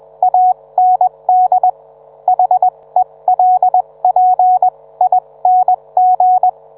Morse code, heard as a keyed tone, about 21 wpm, spaced as for 18 wpm, 740 Hz, with radio noise. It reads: AND HELPING